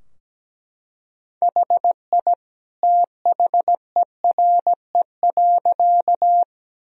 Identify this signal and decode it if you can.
Morse code, heard as a keyed tone, about 17 wpm, 700 Hz. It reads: HI THERE.
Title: Sentences from Top 1000 Words - Review Part 3 - 17wpm